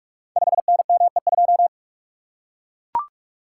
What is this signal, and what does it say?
Morse code, heard as a keyed tone, about 44 wpm, 700 Hz. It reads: 5NME2
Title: Code Groups - 5 Letter:Numbers - 45wpm